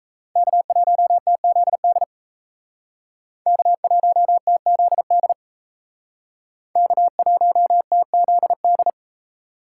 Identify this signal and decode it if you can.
Morse code, heard as a keyed tone, about 36 wpm, 700 Hz. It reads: K1TZD K1TZD K1TZD